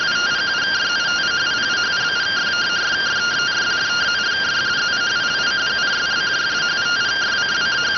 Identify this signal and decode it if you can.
Two-tone teleprinter signal, about 170 Hz shift.